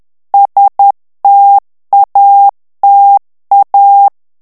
Morse code, heard as a keyed tone, about 10 wpm, 800 Hz. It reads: STATA
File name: ITALIEN texte clair 1 vitesse 10 mots minute